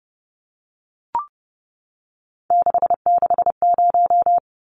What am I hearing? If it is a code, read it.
Morse code, heard as a keyed tone, about 30 wpm, 700 Hz. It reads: 660
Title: Numbers - 3-Digit 30wpm